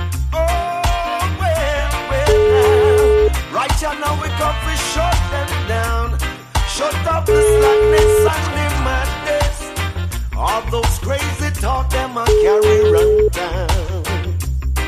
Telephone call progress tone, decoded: ringback tone

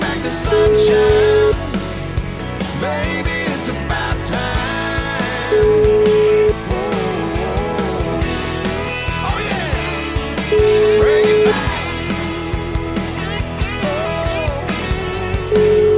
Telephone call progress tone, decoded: ringback tone

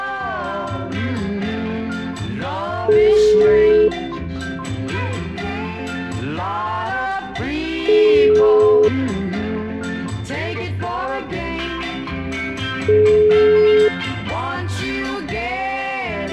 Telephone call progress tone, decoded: ringback tone

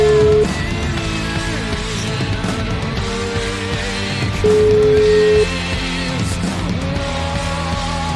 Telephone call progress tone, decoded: ringback tone

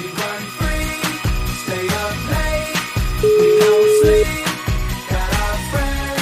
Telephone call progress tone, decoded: ringback tone

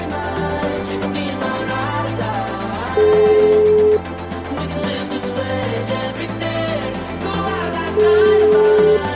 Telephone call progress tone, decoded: ringback tone

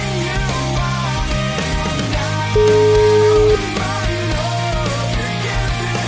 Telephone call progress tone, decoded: ringback tone

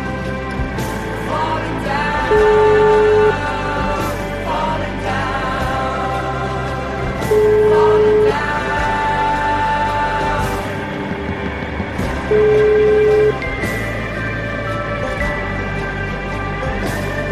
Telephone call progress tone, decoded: ringback tone

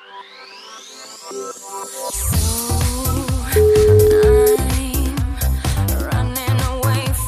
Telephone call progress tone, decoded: ringback tone

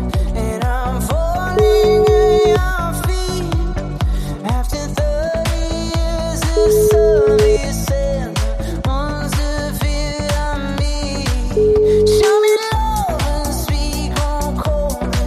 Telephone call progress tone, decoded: ringback tone

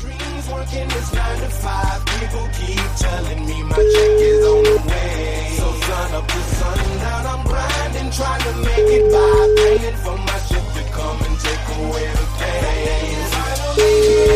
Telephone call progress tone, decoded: ringback tone

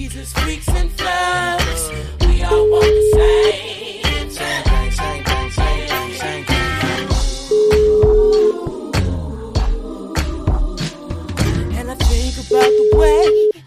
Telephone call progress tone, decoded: ringback tone

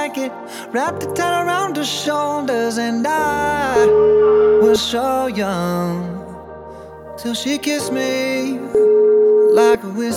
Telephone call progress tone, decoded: ringback tone

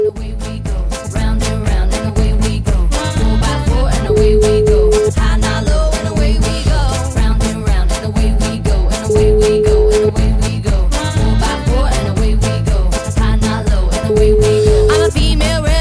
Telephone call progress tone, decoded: ringback tone

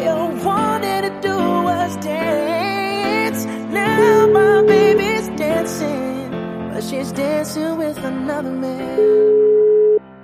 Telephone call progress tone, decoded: ringback tone

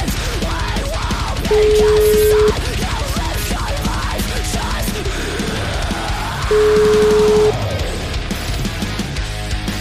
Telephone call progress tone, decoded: ringback tone

